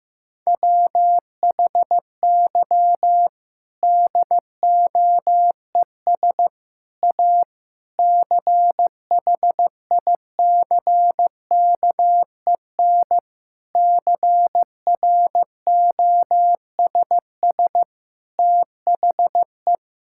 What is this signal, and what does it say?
Morse code, heard as a keyed tone, about 15 wpm, 700 Hz. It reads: WHY DOES A CHICKEN CROSS THE
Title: Sentences from Top 1000 Words - Review Part 5 - 15wpm